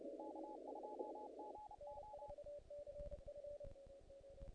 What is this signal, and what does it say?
Morse code, heard as a keyed tone, about 30 wpm, 840 Hz, with noise. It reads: K2GO
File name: contests / naqp-cw-k5tr-op-k5pi- -1900